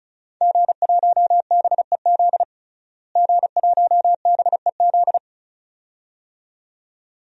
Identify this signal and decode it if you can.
Morse code, heard as a keyed tone, about 36 wpm, 700 Hz. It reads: G1BEZ G1BEZ
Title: DXCC-100 Call Signs - 5 Characters - Repeated 2x - 35wpm